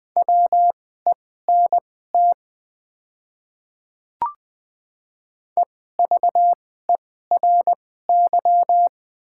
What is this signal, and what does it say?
Morse code, heard as a keyed tone, about 20 wpm, spaced as for 14 wpm, 700 Hz. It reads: WENT EVERY
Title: 2x Character Spacing - Top 300 Words - 20wpm